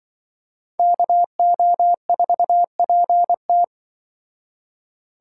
Morse code, heard as a keyed tone, about 24 wpm, 700 Hz. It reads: KO4PT
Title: Speed Racing - USA Call Signs 20wpm